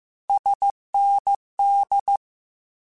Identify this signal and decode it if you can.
Morse code, heard as a keyed tone, about 15 wpm, 800 Hz. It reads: SND
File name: Abreviations / Abreviations mp3s / Abreviations@13